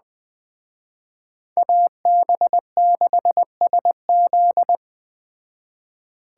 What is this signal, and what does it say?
Morse code, heard as a keyed tone, about 20 wpm, 700 Hz. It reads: AB6SZ
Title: Speed Racing - USA 2x2 Call Signs 17wpm